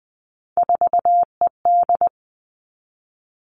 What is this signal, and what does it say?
Morse code, heard as a keyed tone, about 20 wpm, 700 Hz. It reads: 4ED
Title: Three Challenging Letter-Number Combinations 20wpm